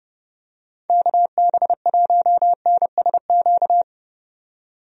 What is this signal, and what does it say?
Morse code, heard as a keyed tone, about 30 wpm, 700 Hz. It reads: KB1NSQ
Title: USA Call Signs - 6 Characters - 30wpm